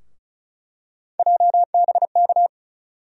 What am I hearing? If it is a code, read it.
Morse code, heard as a keyed tone, about 36 wpm, 700 Hz. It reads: JBK
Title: USA Call Sign Postfixes - 3 Letters 35wpm